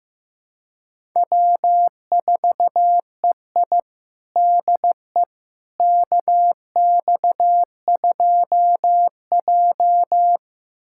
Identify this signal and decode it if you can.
Morse code, heard as a keyed tone, about 16 wpm, 700 Hz. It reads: W4EI DE KX2J
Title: USA Callsign DE Callsign - 4 Characters - 15wpm